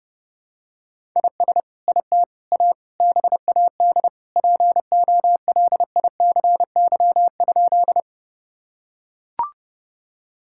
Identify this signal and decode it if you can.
Morse code, heard as a keyed tone, about 32 wpm, 700 Hz. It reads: IS IT A BAD POLICY?
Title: Sentences from Top 1000 Words - Review Part 5 - 30wpm